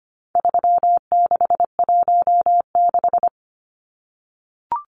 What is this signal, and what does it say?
Morse code, heard as a keyed tone, about 25 wpm, 700 Hz. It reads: 3616
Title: Numbers - 4-Digit 25wpm